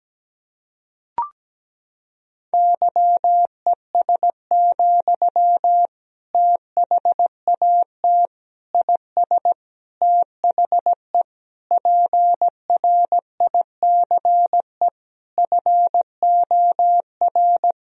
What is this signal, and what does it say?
Morse code, heard as a keyed tone, about 17 wpm, 700 Hz. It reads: YES, THAT IS THE PRICE FOR
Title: Sentences from Top 1000 Words - Review Part 4 - 17wpm